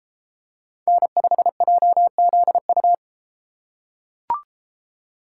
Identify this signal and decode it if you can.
Morse code, heard as a keyed tone, about 31 wpm, 700 Hz. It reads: N5JZU